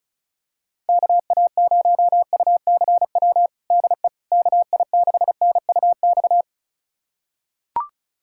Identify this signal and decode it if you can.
Morse code, heard as a keyed tone, about 36 wpm, 700 Hz. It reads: KA0UCW DE KI6NUX
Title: USA Callsign DE Callsign - 6 Characters - 35wpm